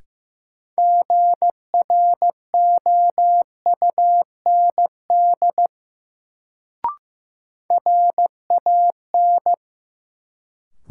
Morse code, heard as a keyed tone, about 15 wpm, 700 Hz. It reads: GROUND RAN
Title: Sentences from Top 500 Words - New Word Drill - 15wpm